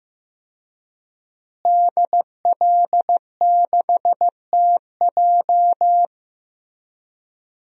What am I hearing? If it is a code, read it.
Morse code, heard as a keyed tone, about 15 wpm, 700 Hz. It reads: DL6TJ
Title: DXCC-100 Call Signs - 5 Characters - Repeated 3x - 15wpm